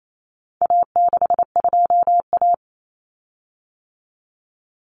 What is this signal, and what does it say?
Morse code, heard as a keyed tone, about 28 wpm, 700 Hz. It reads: A62A
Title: Common Cut Numbers - 4-Digit 28wpm